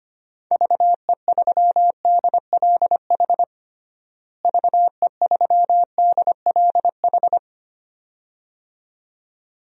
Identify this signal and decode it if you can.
Morse code, heard as a keyed tone, about 25 wpm, 700 Hz. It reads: VE3DLH VE3DLH
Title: DXCC-100 Call Signs - 6 Characters - Repeated 2x - 25wpm